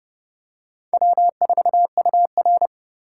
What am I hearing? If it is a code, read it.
Morse code, heard as a keyed tone, about 29 wpm, 700 Hz. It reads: W4UR